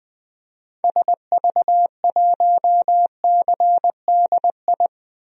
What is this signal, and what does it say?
Morse code, heard as a keyed tone, about 20 wpm, 700 Hz. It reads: SV1CDI
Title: DXCC-100 Call Signs - 6 Characters - 20wpm